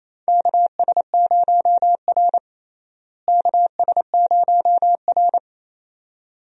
Morse code, heard as a keyed tone, about 28 wpm, 700 Hz. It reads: KS0R KS0R
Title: USA Call Signs - 4 Characters - Repeated 2x - 28wpm